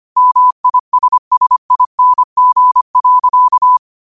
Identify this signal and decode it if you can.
Morse code, heard as a keyed tone, about 25 wpm, 1000 Hz. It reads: MISSING.
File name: 25wpm